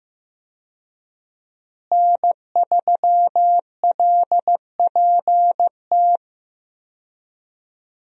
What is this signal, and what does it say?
Morse code, heard as a keyed tone, about 15 wpm, 700 Hz. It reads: N3LPT
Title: USA Call Signs - 5 Characters - Repeated 2x - 15wpm